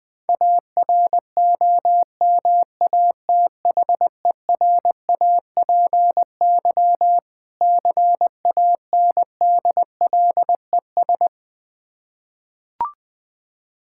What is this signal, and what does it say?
Morse code, heard as a keyed tone, about 20 wpm, 690 Hz. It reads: AROMATHERAPY CANDLES
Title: Difficult Words with Hints - 20wpm